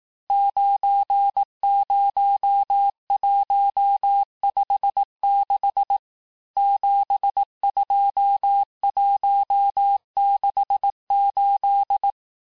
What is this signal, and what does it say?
Morse code, heard as a keyed tone, about 18 wpm, 800 Hz. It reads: 90156 72168
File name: cours-du-lundi files / cours-1594--teste-chiffre-vitesse-17mpm